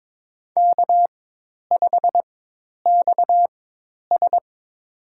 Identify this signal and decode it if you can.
Morse code, heard as a keyed tone, about 22 wpm, spaced as for 10 wpm, 700 Hz. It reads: K5XS
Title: USA Call Signs - 4 Characters - 4x Character Spacing - 22wpm